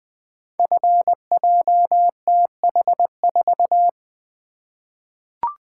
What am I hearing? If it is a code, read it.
Morse code, heard as a keyed tone, about 20 wpm, 700 Hz. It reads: FJTH4
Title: Code Groups - 5 Letter:Numbers - 20wpm